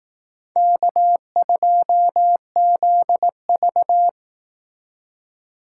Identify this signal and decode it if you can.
Morse code, heard as a keyed tone, about 18 wpm, 700 Hz. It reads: K2ZV
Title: Speed Racing - USA 1x2 Call Signs 15wpm